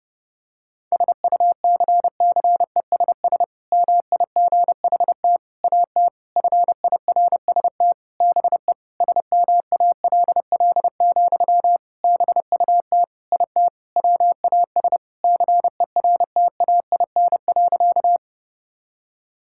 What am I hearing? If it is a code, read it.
Morse code, heard as a keyed tone, about 30 wpm, 700 Hz. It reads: SUCCESS MIGHT AT FIRST BE SMALL, BUT IT WAS CERTAIN.
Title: Sentences from Top 1000 Words - Review Part 6 - 30wpm